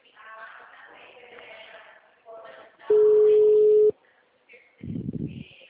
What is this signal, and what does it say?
Ringback tone (Europe, 425 Hz).
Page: Dracena Call